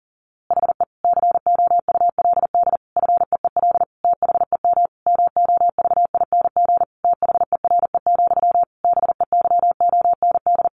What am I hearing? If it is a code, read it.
Morse code, heard as a keyed tone, about 40 wpm, 700 Hz. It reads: HE COULD FEEL THEM MOVING THERE, BEYOND
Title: A Sound Of Thunder 40wpm